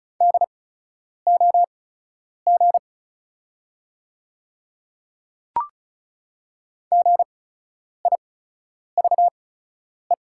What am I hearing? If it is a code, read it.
Morse code, heard as a keyed tone, about 36 wpm, spaced as for 10 wpm, 700 Hz. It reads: DOG GIVE
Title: 8x Character Spacing - Top 500 Words - 35wpm